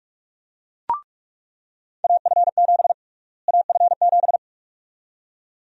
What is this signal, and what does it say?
Morse code, heard as a keyed tone, about 46 wpm, 700 Hz. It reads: AF7 AF7
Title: USA Call Sign Prefixes - Repeated 2x - 45wpm